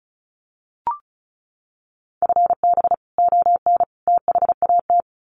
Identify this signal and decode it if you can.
Morse code, heard as a keyed tone, about 34 wpm, 700 Hz. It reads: FB ON THAT
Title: Common QSO Phrases 35wpm